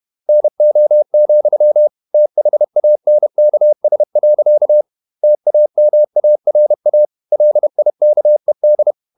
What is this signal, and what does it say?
Morse code, heard as a keyed tone, about 31 wpm, 600 Hz. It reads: NO, THANKS. TAMARA LIKED
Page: Never 31wpm | CW med Gnister